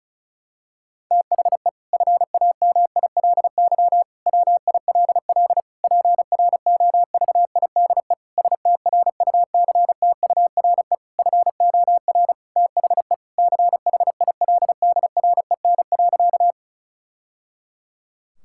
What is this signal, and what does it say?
Morse code, heard as a keyed tone, about 34 wpm, 700 Hz. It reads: THE FAMILY WILL PROVIDE STRUCTURE FOR THE CHILDREN.
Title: Sentences from Top 1000 Words - Review Part 6 - 35wpm